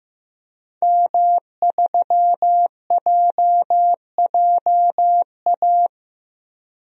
Morse code, heard as a keyed tone, about 15 wpm, 700 Hz. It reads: M3JJA
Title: DXCC-100 Call Signs - 5 Characters - 15wpm